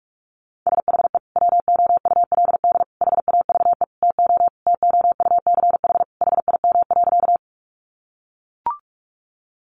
Morse code, heard as a keyed tone, about 44 wpm, 700 Hz. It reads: SHE WOULD HAVE TO TOUCH HIM.
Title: Sentences from Top 1200 Words 45wpm